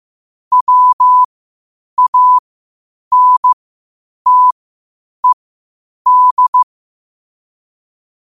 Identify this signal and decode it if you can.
Morse code, heard as a keyed tone, about 15 wpm, spaced as for 8 wpm, 1000 Hz. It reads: WANTED
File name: qotdfiles / 8wpm